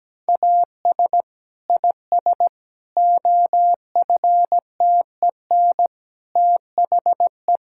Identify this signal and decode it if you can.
Morse code, heard as a keyed tone, about 17 wpm, 700 Hz. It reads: AS IS OFTEN THE
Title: Sentences from Top 1000 Words - Review Part 5 - 17wpm